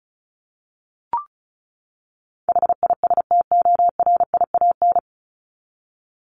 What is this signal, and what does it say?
Morse code, heard as a keyed tone, about 34 wpm, 700 Hz. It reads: HISTORIAN
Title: Top 1751-2000 Words 35wpm